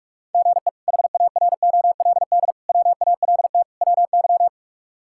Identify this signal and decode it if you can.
Morse code, heard as a keyed tone, about 46 wpm, 700 Hz. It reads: GE HAROLD WALT WY
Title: K1USN SST QSO Practice - 45wpm